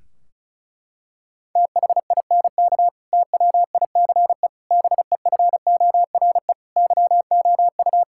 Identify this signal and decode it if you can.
Morse code, heard as a keyed tone, about 34 wpm, 700 Hz. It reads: THINK TWICE BEFORE YOU
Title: Sentences from Top 1600 Words 35wpm